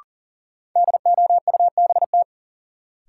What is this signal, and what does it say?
Morse code, heard as a keyed tone, about 40 wpm, 700 Hz. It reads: DOUBT
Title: Sets of 5 Words - New Word Drill - 40wpm